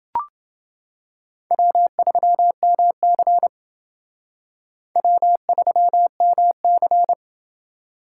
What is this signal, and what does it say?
Morse code, heard as a keyed tone, about 29 wpm, 700 Hz. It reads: W3MC W3MC